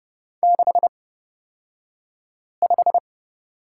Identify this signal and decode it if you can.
Morse code, heard as a keyed tone, about 30 wpm, spaced as for 5 wpm, 700 Hz. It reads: 65
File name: Lesson 36d - BT - Two Characters 30wpm